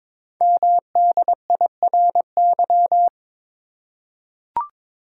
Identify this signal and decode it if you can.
Morse code, heard as a keyed tone, about 22 wpm, 690 Hz. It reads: MDIRY
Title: Code Groups - 5 Letters - 22wpm